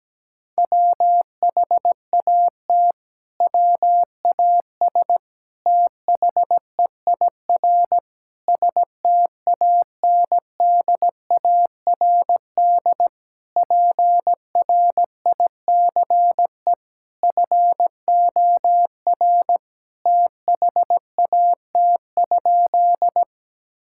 Morse code, heard as a keyed tone, about 17 wpm, 700 Hz. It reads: WHAT WAS THEIR STANDARD PRICE FOR THAT?
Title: Sentences from Top 1000 Words - Review Part 6 - 17wpm